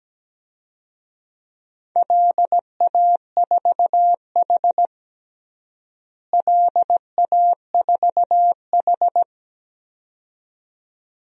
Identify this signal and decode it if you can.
Morse code, heard as a keyed tone, about 17 wpm, 700 Hz. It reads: LA4H LA4H
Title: DXCC-100 Call Signs - 4 Characters - Repeated 2x - 17wpm